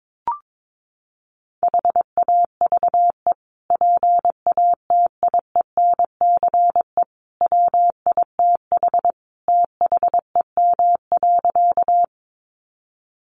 Morse code, heard as a keyed tone, about 22 wpm, 700 Hz. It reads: HAVE PATIENCE WITH THEM.
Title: Sentences from Top 1300 Words 22wpm